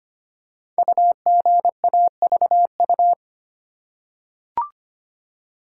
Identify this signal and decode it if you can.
Morse code, heard as a keyed tone, about 25 wpm, 700 Hz. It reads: UGAVU
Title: Code Groups - 5 Letter:Numbers - 25wpm